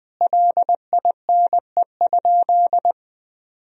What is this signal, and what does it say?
Morse code, heard as a keyed tone, about 20 wpm, 700 Hz. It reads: LINE?
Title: Sentences from Top 1000 Words - Review Part 1 - 20wpm